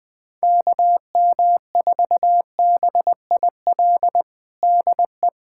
Morse code, heard as a keyed tone, about 20 wpm, 700 Hz. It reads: KM4BIL DE
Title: USA Callsign DE Callsign - 6 Characters - 20wpm